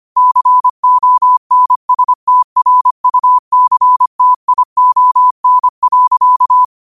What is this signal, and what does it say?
Morse code, heard as a keyed tone, about 25 wpm, 1000 Hz. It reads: CONSTRUCTION.
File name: qotdfiles / 25wpm